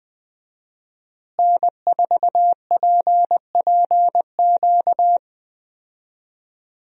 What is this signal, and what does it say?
Morse code, heard as a keyed tone, about 20 wpm, 700 Hz. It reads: N4PPQ
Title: USA Call Signs - 5 Characters - Repeated 3x - 20wpm